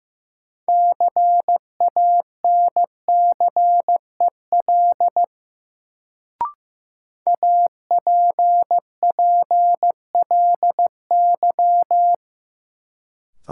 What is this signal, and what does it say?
Morse code, heard as a keyed tone, about 15 wpm, 700 Hz. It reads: CANCEL APPLY
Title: Sentences from Top 1200 Words - New Word Drill Alt - 15wpm